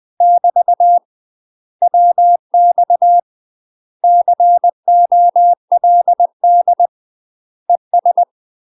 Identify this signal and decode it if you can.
Morse code, heard as a keyed tone, about 20 wpm, 700 Hz. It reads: <BT> WX COLD ES
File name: UK MASTER QSOs TRAINING/20WPM